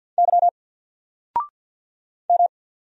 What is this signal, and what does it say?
Morse code, heard as a keyed tone, about 50 wpm, 700 Hz. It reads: Y M
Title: Single Letter-Number 50wpm